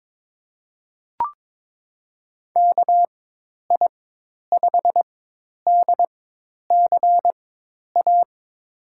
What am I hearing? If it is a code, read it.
Morse code, heard as a keyed tone, about 22 wpm, spaced as for 10 wpm, 700 Hz. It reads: KI5DCA
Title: USA Call Signs - 6 Characters - 4x Character Spacing - 22wpm